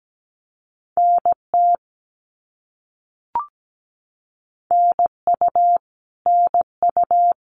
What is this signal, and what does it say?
Morse code, heard as a keyed tone, about 17 wpm, 700 Hz. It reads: NT NU NU
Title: Canadian Provinces and Territories - Repeated 3x - 17wpm